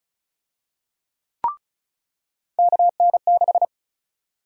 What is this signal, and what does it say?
Morse code, heard as a keyed tone, about 34 wpm, 700 Hz. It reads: KN6
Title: USA Call Sign Prefixes - 35wpm